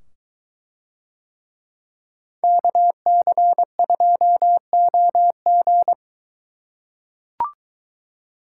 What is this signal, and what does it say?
Morse code, heard as a keyed tone, about 21 wpm, 700 Hz. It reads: KC2OG